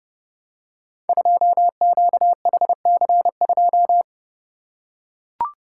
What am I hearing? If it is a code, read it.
Morse code, heard as a keyed tone, about 32 wpm, 700 Hz. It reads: 2QHC2
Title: Code Groups - 5 Letter:Numbers - 30wpm